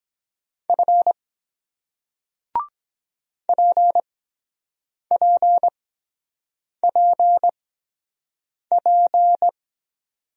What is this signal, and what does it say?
Morse code, heard as a keyed tone, about 23 wpm, spaced as for 7 wpm, 700 Hz. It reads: F PPPP